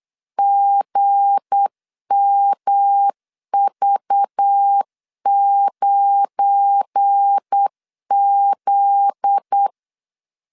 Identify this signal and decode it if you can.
Morse code, heard as a keyed tone, about 8 wpm, 800 Hz. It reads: GMV9Z
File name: LEZIONI / 8 YJ45 / 8 YJ45 8wpm 3214